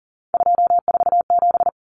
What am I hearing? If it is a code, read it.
Morse code, heard as a keyed tone, about 40 wpm, 700 Hz. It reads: 247
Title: Numbers - 3-Digit 40wpm